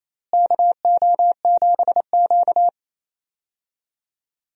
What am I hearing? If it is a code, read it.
Morse code, heard as a keyed tone, about 28 wpm, 700 Hz. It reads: KO7Q
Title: USA Call Signs - 4 Characters - Repeated 2x - 28wpm